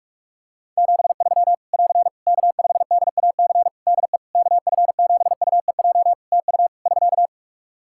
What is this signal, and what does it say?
Morse code, heard as a keyed tone, about 48 wpm, 700 Hz. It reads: 73 <AR> K5DAC DE KF7UEJ TU <SK>
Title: Practice QSO - Exchange 4 - Ending - 45wpm